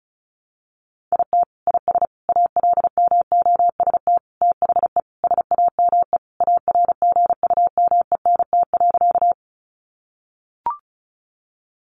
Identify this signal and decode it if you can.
Morse code, heard as a keyed tone, about 34 wpm, 700 Hz. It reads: IT IS ALMOST THE SAME ARGUMENT.
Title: Sentences from Top 1400 Words 35wpm